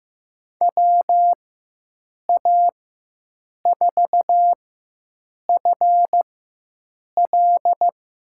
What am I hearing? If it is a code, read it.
Morse code, heard as a keyed tone, about 15 wpm, spaced as for 7 wpm, 700 Hz. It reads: WA4FL
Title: USA Call Signs - 6 Characters - 4x Character Spacing - 15wpm